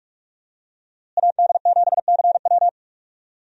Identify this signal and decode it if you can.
Morse code, heard as a keyed tone, about 46 wpm, 700 Hz. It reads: AD7CW
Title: USA Call Signs - 5 Characters - Repeated 3x - 45wpm